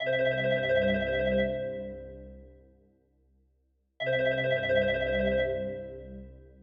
There is no code.